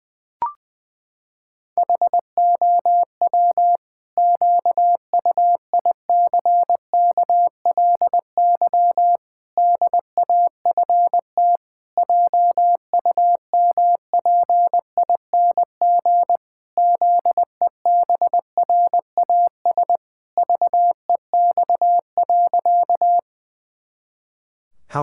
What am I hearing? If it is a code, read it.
Morse code, heard as a keyed tone, about 20 wpm, 700 Hz. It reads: HOW QUICKLY DAFT JUMPING ZEBRAS VEX.
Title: Pangram Challenge v2 - 20wpm